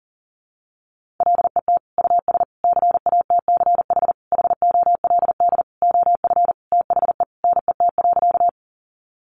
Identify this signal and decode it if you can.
Morse code, heard as a keyed tone, about 40 wpm, 700 Hz. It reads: LET US CATCH HOLD OF THE NET.
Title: Sentences from Top 1500 Words 40wpm